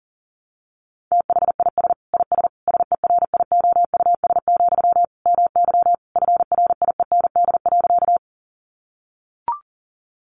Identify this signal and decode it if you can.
Morse code, heard as a keyed tone, about 40 wpm, 700 Hz. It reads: THIS IS SERIOUS, MY FRIEND.
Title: Sentences from Top 1100 Words 40wpm